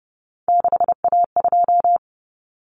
Morse code, heard as a keyed tone, about 30 wpm, 700 Hz. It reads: 6A2
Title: Common Cut Numbers - 3-Digit 30wpm